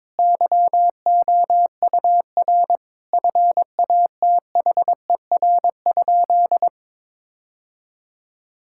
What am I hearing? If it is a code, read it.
Morse code, heard as a keyed tone, about 22 wpm, 700 Hz. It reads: YOUR FATHER?
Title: Sentences from Top 1000 Words - Review Part 5 - 22wpm